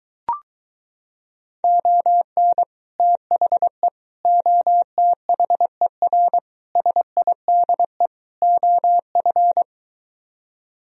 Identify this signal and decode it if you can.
Morse code, heard as a keyed tone, about 24 wpm, 700 Hz. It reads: ON THE OTHER SIDE OF